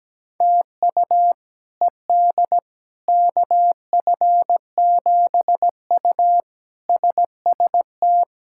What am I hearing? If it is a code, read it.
Morse code, heard as a keyed tone, about 17 wpm, 700 Hz. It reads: TU ED KF7U SST
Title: K1USN SST QSO Practice - 17wpm